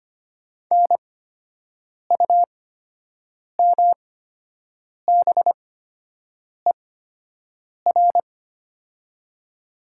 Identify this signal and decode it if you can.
Morse code, heard as a keyed tone, about 25 wpm, spaced as for 7 wpm, 700 Hz. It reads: NUMBER
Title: 8x Character Spacing - Top 200 Words - 25wpm